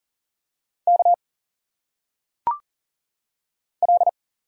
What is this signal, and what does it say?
Morse code, heard as a keyed tone, about 40 wpm, 700 Hz. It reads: K L
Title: Single Letters - Phonetic Pronunciation 40wpm